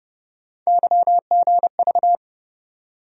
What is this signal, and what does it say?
Morse code, heard as a keyed tone, about 30 wpm, 700 Hz. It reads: YGV
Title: USA Call Sign Postfixes - 3 Letters 30wpm